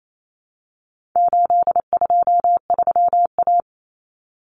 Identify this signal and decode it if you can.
Morse code, heard as a keyed tone, about 28 wpm, 700 Hz. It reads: 823A
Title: Common Cut Numbers - 4-Digit 28wpm